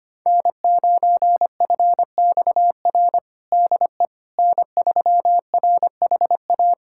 Morse code, heard as a keyed tone, about 25 wpm, 700 Hz. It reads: N9FXR DE N3RHA
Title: USA Callsign DE Callsign - 5 Characters - 25wpm